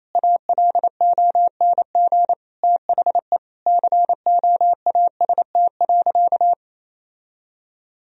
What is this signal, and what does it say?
Morse code, heard as a keyed tone, about 28 wpm, 700 Hz. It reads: ALONG THE COAST.
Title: Sentences from Top 1000 Words - Review Part 4 - 28wpm